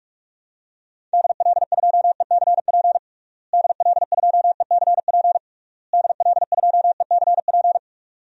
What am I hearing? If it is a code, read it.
Morse code, heard as a keyed tone, about 46 wpm, 700 Hz. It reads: DL2ECP DL2ECP DL2ECP
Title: DXCC-100 Call Signs - 6 Characters - Repeated 3x - 45wpm